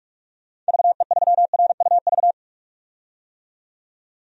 Morse code, heard as a keyed tone, about 46 wpm, 700 Hz. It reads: VE3RUV
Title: DXCC-100 Call Signs - 6 Characters - 45wpm